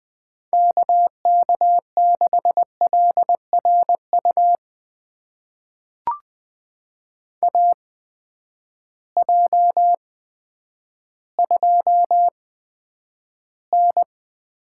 Morse code, heard as a keyed tone, about 20 wpm, spaced as for 10 wpm, 700 Hz. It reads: KK6LRU AJ2N
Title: USA Call Signs - 8x Character Spacing - 20wpm